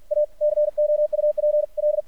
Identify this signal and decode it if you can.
Morse code, heard as a keyed tone, about 53 wpm, 600 Hz. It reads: A COUPL
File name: archived / CRSnet